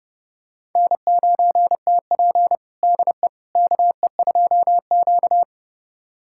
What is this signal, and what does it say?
Morse code, heard as a keyed tone, about 30 wpm, 700 Hz. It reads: N9TP DE KE2Q
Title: USA Callsign DE Callsign - 4 Characters - 30wpm